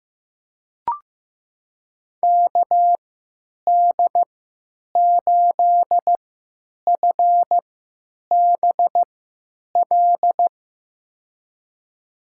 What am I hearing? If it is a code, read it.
Morse code, heard as a keyed tone, about 15 wpm, spaced as for 9 wpm, 700 Hz. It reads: KD8FBL
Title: USA Call Signs - 3x Character Spacing - 15wpm